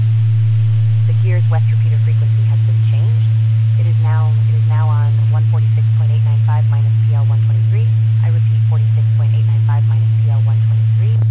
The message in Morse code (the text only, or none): none